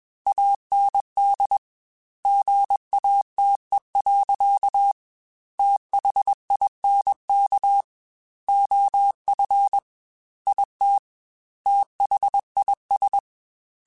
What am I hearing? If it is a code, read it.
Morse code, heard as a keyed tone, about 21 wpm, 800 Hz. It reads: AND GATE. THINK OF IT THIS
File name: Technical Subjects / Technical Subjects-mp3s / OR Gate @20